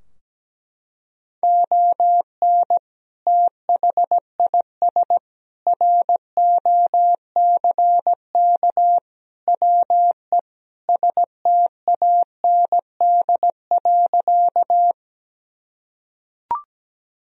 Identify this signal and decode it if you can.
Morse code, heard as a keyed tone, about 17 wpm, 700 Hz. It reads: ON THIS ROCK WE STAND.
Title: Sentences from Top 1000 Words - Review Part 2 - 17wpm